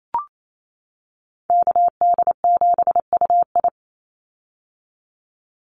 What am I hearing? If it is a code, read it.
Morse code, heard as a keyed tone, about 28 wpm, 700 Hz. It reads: KD7UI
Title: Call Signs - 28wpm